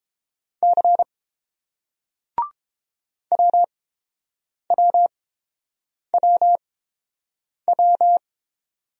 Morse code, heard as a keyed tone, about 31 wpm, spaced as for 7 wpm, 700 Hz. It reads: C WWWW